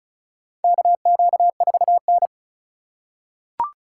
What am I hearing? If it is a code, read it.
Morse code, heard as a keyed tone, about 36 wpm, 700 Hz. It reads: KQ4N
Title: USA Call Signs - 4 Characters - 3x Character Spacing - 35wpm